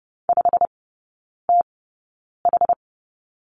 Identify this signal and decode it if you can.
Morse code, heard as a keyed tone, about 30 wpm, spaced as for 9 wpm, 700 Hz. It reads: 5TH
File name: Lesson 18e - Letter m - Three Characters 30wpm